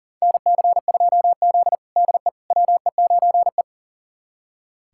Morse code, heard as a keyed tone, about 40 wpm, 700 Hz. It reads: NC2Z DE WE9E
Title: USA Callsign DE Callsign - 4 Characters - 40wpm